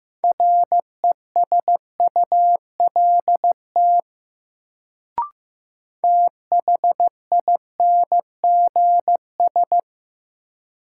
Morse code, heard as a keyed tone, about 15 wpm, 700 Hz. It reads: RESULT THINGS
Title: Sets of 3 Words - New Word Drill - 15wpm